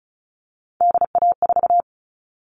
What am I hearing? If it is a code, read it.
Morse code, heard as a keyed tone, about 34 wpm, 700 Hz. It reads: DA4
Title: All Cut Numbers - 3-Digit 35wpm